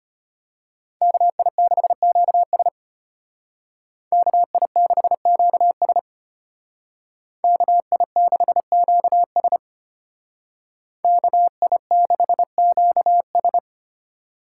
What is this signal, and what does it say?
Morse code, heard as a keyed tone, about 30 wpm, 700 Hz. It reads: KI6QS KI6QS KI6QS KI6QS
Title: Speed Racing - USA 2x2 Call Signs 25wpm